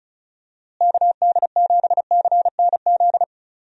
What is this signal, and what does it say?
Morse code, heard as a keyed tone, about 36 wpm, 700 Hz. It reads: KD7CNZ
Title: USA Call Signs - 6 Characters - Repeated 2x - 35wpm